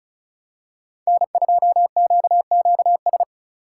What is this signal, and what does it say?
Morse code, heard as a keyed tone, about 36 wpm, 700 Hz. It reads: N2QQS
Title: USA Call Signs - 5 Characters - 4x Character Spacing - 35wpm